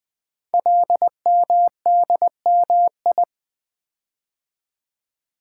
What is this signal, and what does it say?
Morse code, heard as a keyed tone, about 20 wpm, 700 Hz. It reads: LMDMI
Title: Code Groups - 5 Letters - 20wpm